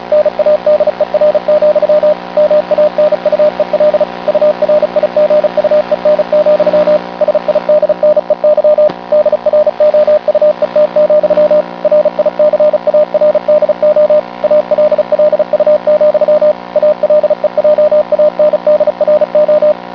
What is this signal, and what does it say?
Morse code, heard as a keyed tone, about 36 wpm, 600 Hz, with radio noise. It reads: NADER, MANUEL URIGUEN, SIDNEY DROUET, RICARDO ALLU, ALEJANDRO